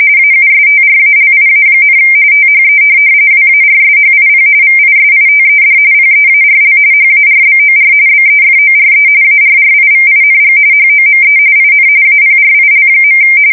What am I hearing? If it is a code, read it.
RTTY, about 45.45 baud. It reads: AUSTRALIA, THE WARSHIP VISITED PEARL HARBOR BEFORE ARRIVING HOME IN APRIL.
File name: fsk170 / USS-Missouri